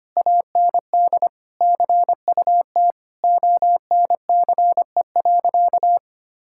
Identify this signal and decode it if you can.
Morse code, heard as a keyed tone, about 25 wpm, 700 Hz. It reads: AND CUT ONCE.
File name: practice / 25wpm / Difficult Sentences with Hints - 25wpm